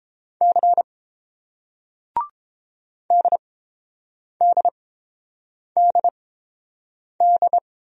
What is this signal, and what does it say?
Morse code, heard as a keyed tone, about 30 wpm, spaced as for 7 wpm, 700 Hz. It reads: C DDDD